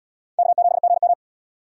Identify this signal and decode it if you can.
Morse code, heard as a keyed tone, about 77 wpm, 700 Hz. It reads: K7GM